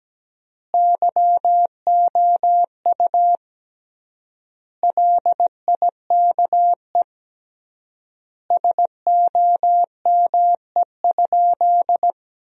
Morse code, heard as a keyed tone, about 17 wpm, 700 Hz. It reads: YOU LIKE SOME?
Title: 3x Word-Spacing - Sentences from Top 100 Words - 17wpm